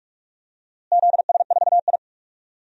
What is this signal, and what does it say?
Morse code, heard as a keyed tone, about 42 wpm, 700 Hz. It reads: ZS4I
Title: DXCC-100 Call Signs - 4 Characters - Repeated 2x - 45wpm